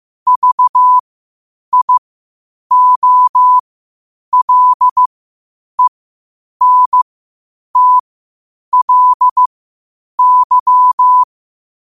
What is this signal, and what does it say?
Morse code, heard as a keyed tone, about 15 wpm, spaced as for 8 wpm, 1000 Hz. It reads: VIOLENTLY
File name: qotdfiles / 8wpm